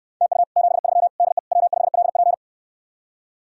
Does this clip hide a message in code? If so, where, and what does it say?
Yes: Morse code from 0.2 to 2.3 s, heard as a keyed tone, about 69 wpm, 700 Hz. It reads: TU 73 DE W5LP